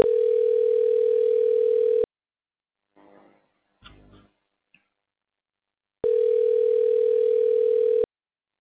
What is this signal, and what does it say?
Ringback tone (North America), 440 + 480 Hz.